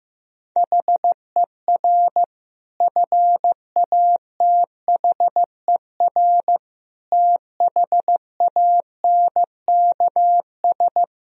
Morse code, heard as a keyed tone, about 15 wpm, 700 Hz. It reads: HER FATHER THANKS
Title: Sentences from Top 1000 Words - Review Part 3 - 15wpm